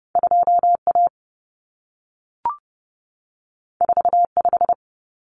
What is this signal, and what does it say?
Morse code, heard as a keyed tone, about 30 wpm, 700 Hz. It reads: 2A 45
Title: Common Cut Numbers - 2-Digit 30wpm